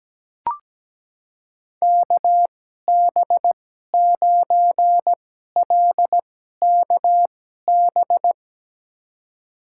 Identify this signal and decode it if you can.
Morse code, heard as a keyed tone, about 17 wpm, spaced as for 12 wpm, 700 Hz. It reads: KB9LKB
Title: USA Call Signs - 6 Characters - 2x Character Spacing - 17wpm